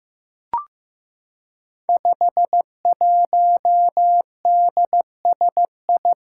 Morse code, heard as a keyed tone, about 15 wpm, 700 Hz. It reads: 51DSI
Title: Five Challenging Letter-Number Combinations 15wpm